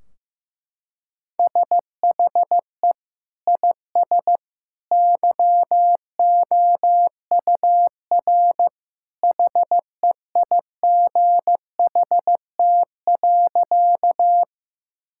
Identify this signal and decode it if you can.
Morse code, heard as a keyed tone, about 15 wpm, 700 Hz. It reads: SHE IS YOUR HEIGHT.
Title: Sentences from Top 1600 Words 15wpm